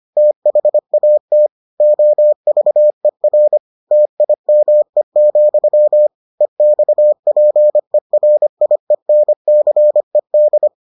Morse code, heard as a keyed tone, about 25 wpm, 600 Hz. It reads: THAT OVER TIME, EXPERIENCED